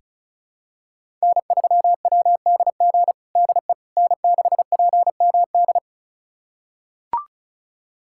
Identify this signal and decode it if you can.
Morse code, heard as a keyed tone, about 36 wpm, 700 Hz. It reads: N3WDG DE N6PMD
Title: USA Callsign DE Callsign - 5 Characters - 35wpm